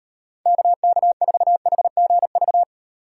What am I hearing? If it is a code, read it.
Morse code, heard as a keyed tone, about 38 wpm, 700 Hz. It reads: KK4HGV